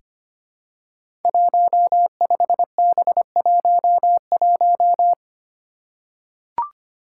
Morse code, heard as a keyed tone, about 25 wpm, 700 Hz. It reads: 15B11
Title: Five Challenging Letter-Number Combinations 25wpm